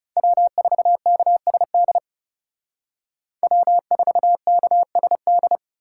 Morse code, heard as a keyed tone, about 38 wpm, 700 Hz. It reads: W4KSD W4KSD